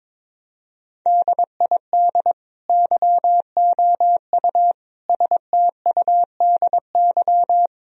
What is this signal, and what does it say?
Morse code, heard as a keyed tone, about 22 wpm, 700 Hz. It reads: DID YOU STUDY
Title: Sentences from Top 1000 Words - Review Part 4 - 22wpm